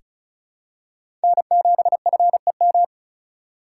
Morse code, heard as a keyed tone, about 36 wpm, 700 Hz. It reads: N7FEM
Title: USA Call Signs - 5 Characters - 2x Character Spacing - 35wpm